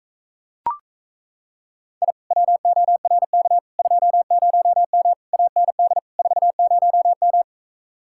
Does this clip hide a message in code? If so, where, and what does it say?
Yes: Morse code from 2.0 to 7.4 s, heard as a keyed tone, about 37 wpm, 700 Hz. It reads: I WORK 20M AND 40M